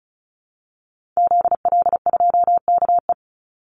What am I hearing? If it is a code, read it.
Morse code, heard as a keyed tone, about 34 wpm, 700 Hz. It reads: ZL2KE
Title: Call Signs Repeated 3x - 35wpm